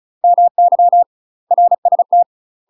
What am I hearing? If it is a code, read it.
Morse code, heard as a keyed tone, about 34 wpm, 700 Hz. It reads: MY RST
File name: UK MASTER QSOs TRAINING/35WPM